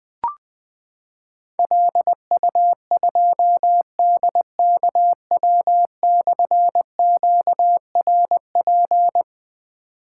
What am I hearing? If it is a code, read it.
Morse code, heard as a keyed tone, about 21 wpm, 700 Hz. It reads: LU2DKW/QRP
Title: Difficult Call Signs 20wpm